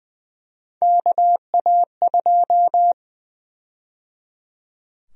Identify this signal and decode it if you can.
Morse code, heard as a keyed tone, about 20 wpm, 700 Hz. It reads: KA2
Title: USA Call Sign Prefixes - 20wpm